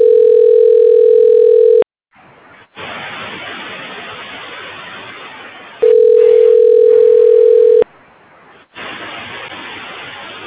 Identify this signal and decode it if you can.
Ringback tone (North America), 440 + 480 Hz.
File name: recordings / REa3766e3cafd471f3a40a13562b733a20